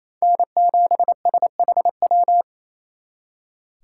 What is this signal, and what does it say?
Morse code, heard as a keyed tone, about 36 wpm, 700 Hz. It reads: N7SHW